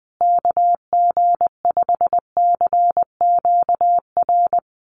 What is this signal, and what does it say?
Morse code, heard as a keyed tone, about 20 wpm, 700 Hz. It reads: KG5CQR
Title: USA Call Signs Repeated 3x 20wpm